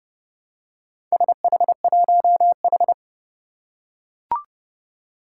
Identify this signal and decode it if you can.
Morse code, heard as a keyed tone, about 30 wpm, 690 Hz. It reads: SH1H
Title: Four Challenging Letter-Number Combinations 30wpm